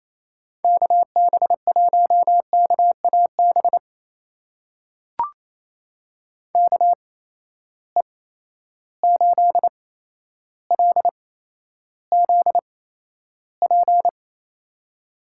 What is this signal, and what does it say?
Morse code, heard as a keyed tone, about 28 wpm, spaced as for 12 wpm, 700 Hz. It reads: KB1KAB KE8LZP
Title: USA Call Signs - 6 Characters - 8x Character Spacing - 28wpm